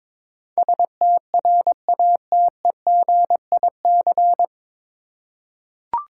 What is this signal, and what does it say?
Morse code, heard as a keyed tone, about 21 wpm, 700 Hz. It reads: STRATEGIC
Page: Morse Code Ninja practice 22wpm